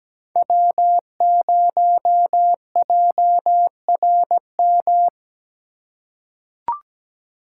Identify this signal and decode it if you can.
Morse code, heard as a keyed tone, about 17 wpm, 700 Hz. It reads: W0JRM
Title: USA Call Signs - 5 Characters - 3x Character Spacing - 17wpm